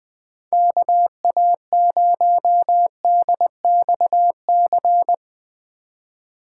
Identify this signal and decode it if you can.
Morse code, heard as a keyed tone, about 22 wpm, 700 Hz. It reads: KA0DXC
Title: Speed Racing - USA Call Signs 20wpm